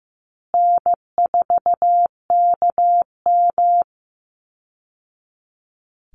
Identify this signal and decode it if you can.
Morse code, heard as a keyed tone, about 15 wpm, 700 Hz. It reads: N4KM
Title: Call Signs Repeated 2x - 15wpm